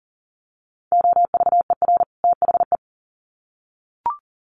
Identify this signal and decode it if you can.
Morse code, heard as a keyed tone, about 40 wpm, 700 Hz. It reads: OVER THE
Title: Sets of 2 Words 40wpm